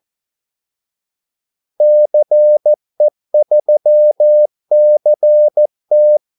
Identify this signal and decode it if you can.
Morse code, heard as a keyed tone, about 14 wpm, 600 Hz. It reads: CE3CT
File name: Callsigns-14wpm